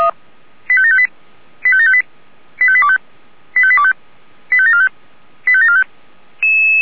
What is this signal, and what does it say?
Five-tone selective call (ZVEI), heard as quick runs of five tones, tones of about 70 ms. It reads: 85758 85758 85825 85825 85745 85745